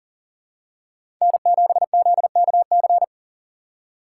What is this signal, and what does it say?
Morse code, heard as a keyed tone, about 40 wpm, 700 Hz. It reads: N7ZKC
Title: USA Call Signs - 5 Characters - 4x Character Spacing - 40wpm